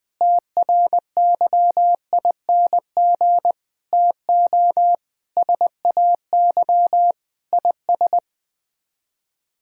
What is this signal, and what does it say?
Morse code, heard as a keyed tone, about 20 wpm, 700 Hz. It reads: TRYING TO SAY IS
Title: Sets of 8 Words - 20wpm